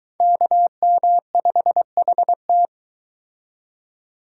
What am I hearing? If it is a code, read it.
Morse code, heard as a keyed tone, about 24 wpm, 700 Hz. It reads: KM5HT